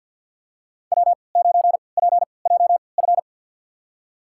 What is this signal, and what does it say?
Morse code, heard as a keyed tone, about 52 wpm, spaced as for 28 wpm, 700 Hz. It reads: W9PJF
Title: USA Call Signs - 5 Characters - 3x Character Spacing - 50wpm